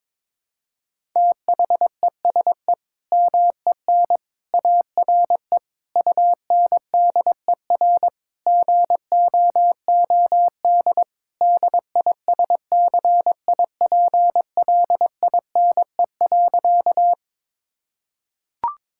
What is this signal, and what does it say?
Morse code, heard as a keyed tone, about 21 wpm, 700 Hz. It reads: THESE MEN ARE UNDER GOOD DISCIPLINE.
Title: Sentences from Top 1000 Words - Review Part 6 - 22wpm